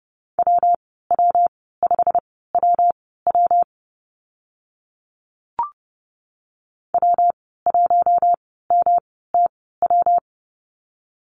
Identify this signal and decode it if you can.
Morse code, heard as a keyed tone, about 30 wpm, spaced as for 17 wpm, 700 Hz. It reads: WW5WW W1MTW
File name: Lesson 19g - Letter w - Call Signs 30wpm